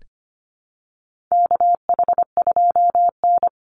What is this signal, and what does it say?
Morse code, heard as a keyed tone, about 25 wpm, 700 Hz. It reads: KH2N
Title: Call Signs - 25wpm